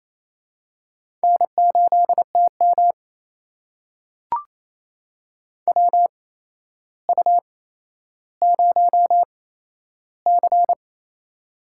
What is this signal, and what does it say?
Morse code, heard as a keyed tone, about 28 wpm, spaced as for 12 wpm, 700 Hz. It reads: N8TM WU0C
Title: USA Call Signs - 4 Characters - 8x Character Spacing - 28wpm